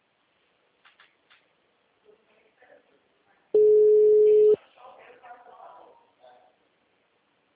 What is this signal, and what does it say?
Ringback tone (Europe, 425 Hz), with speech.